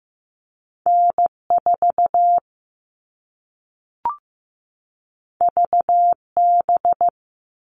Morse code, heard as a keyed tone, about 15 wpm, 700 Hz. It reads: N4 VB
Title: All Cut Numbers - 2-Digit 15wpm